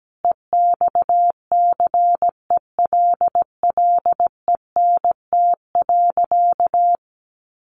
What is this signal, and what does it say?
Morse code, heard as a keyed tone, about 17 wpm, 700 Hz. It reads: EXCELLENT.
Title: Sentences from Top 1000 Words 17wpm